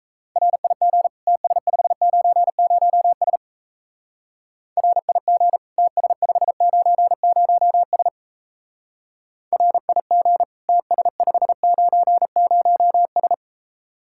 Morse code, heard as a keyed tone, about 34 wpm, 700 Hz. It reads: RIG TS590S RIG TS590S RIG TS590S
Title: Speed-Racing Common QSO Phrases - 28wpm